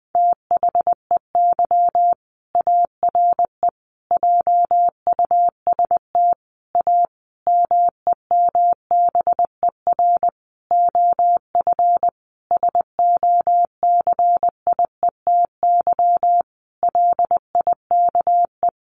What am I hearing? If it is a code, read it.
Morse code, heard as a keyed tone, about 20 wpm, 700 Hz. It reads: THEY ARE JUST A MEMBER OF SOCIETY LIKE
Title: Sentences from Top 1100 Words 20wpm